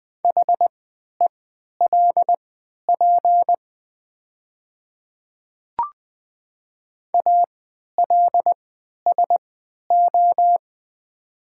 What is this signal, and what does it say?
Morse code, heard as a keyed tone, about 20 wpm, spaced as for 11 wpm, 700 Hz. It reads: HELP ALSO
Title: 3x Character Spacing - Top 200 Words - 20wpm